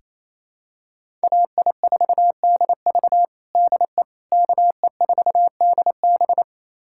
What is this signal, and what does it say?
Morse code, heard as a keyed tone, about 28 wpm, 700 Hz. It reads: AI4DV DE KE4DB
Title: USA Callsign DE Callsign - 5 Characters - 28wpm